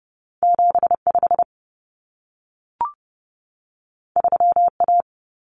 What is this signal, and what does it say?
Morse code, heard as a keyed tone, about 30 wpm, 700 Hz. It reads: 75 3A
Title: Common Cut Numbers - 2-Digit 30wpm